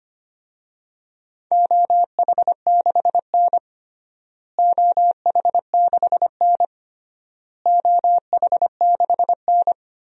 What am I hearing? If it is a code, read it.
Morse code, heard as a keyed tone, about 25 wpm, 700 Hz. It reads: OH6N OH6N OH6N
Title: DXCC-100 Call Signs - 4 Characters - Repeated 3x - 25wpm